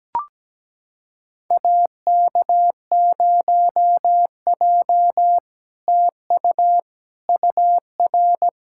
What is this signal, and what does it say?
Morse code, heard as a keyed tone, about 17 wpm, 700 Hz. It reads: AK0J TU UR
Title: SOTA QSO Practice 17wpm